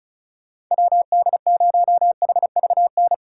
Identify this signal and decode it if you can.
Morse code, heard as a keyed tone, about 44 wpm, 700 Hz. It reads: WD0HVN